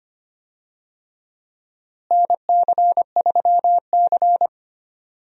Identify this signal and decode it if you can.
Morse code, heard as a keyed tone, about 25 wpm, 700 Hz. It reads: NC3C
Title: USA Call Signs - 4 Characters - Repeated 3x - 25wpm